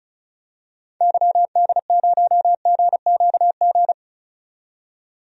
Morse code, heard as a keyed tone, about 34 wpm, 700 Hz. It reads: YD0GQG
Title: DXCC-100 Call Signs - 6 Characters - 35wpm